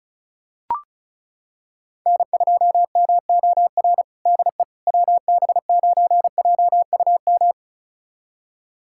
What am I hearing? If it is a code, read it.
Morse code, heard as a keyed tone, about 36 wpm, 700 Hz. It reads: N2MOR DE WB9JUM
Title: USA Callsign DE Callsign 35wpm